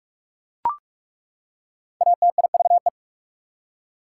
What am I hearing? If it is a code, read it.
Morse code, heard as a keyed tone, about 41 wpm, 690 Hz. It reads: ATIVE
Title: Prefixes and Suffixes 45wpm